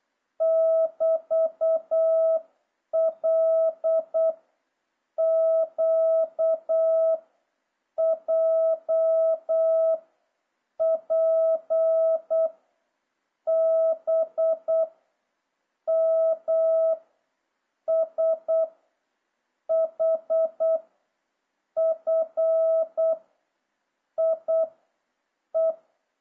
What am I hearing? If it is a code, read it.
Morse code, heard as a keyed tone, about 8 wpm, 650 Hz. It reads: <BT> L Q J P B M S H F I E